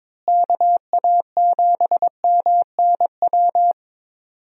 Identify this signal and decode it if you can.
Morse code, heard as a keyed tone, about 22 wpm, 700 Hz. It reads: KA7MNW
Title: USA Call Signs - 6 Characters - Repeated 3x - 22wpm